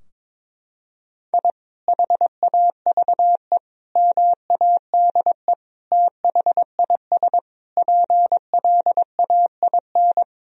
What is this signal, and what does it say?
Morse code, heard as a keyed tone, about 22 wpm, 700 Hz. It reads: I HAVE MADE THIS PLAIN
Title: Sentences from Top 1000 Words - Review Part 2 - 22wpm